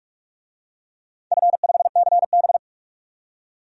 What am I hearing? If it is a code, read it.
Morse code, heard as a keyed tone, about 46 wpm, 700 Hz. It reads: F5CB
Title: DXCC-100 Call Signs - 4 Characters - 45wpm